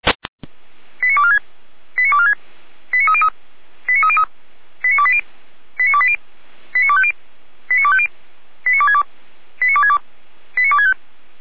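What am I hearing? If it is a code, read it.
Five-tone selective call (ZVEI), heard as quick runs of five tones, tones of about 70 ms. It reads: 89246 89246 89303 89303 89280 89280 89250 89250 89262 89262 89265
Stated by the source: ZVEI five-tone selective call 89229 received with an SDR.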